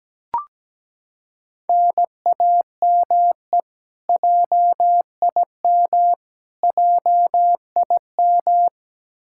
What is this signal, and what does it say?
Morse code, heard as a keyed tone, about 17 wpm, 700 Hz. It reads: NAME JIM JIM
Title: Practice QSO - Exchange 1 - Beginning - 17wpm